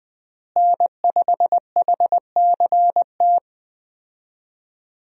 Morse code, heard as a keyed tone, about 20 wpm, 700 Hz. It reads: N5HCT
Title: USA Call Signs - 5 Characters - Repeated 3x - 20wpm